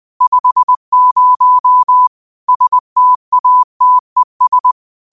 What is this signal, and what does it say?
Morse code, heard as a keyed tone, about 20 wpm, 1000 Hz. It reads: 50 STATES
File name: qotdfiles / 20wpm